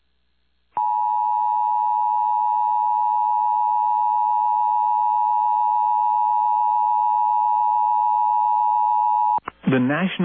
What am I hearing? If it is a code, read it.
Emergency Alert System (SAME) tones (the 853 and 960 Hz attention tone), with speech. It is heard from 0.7 to 9.5 s.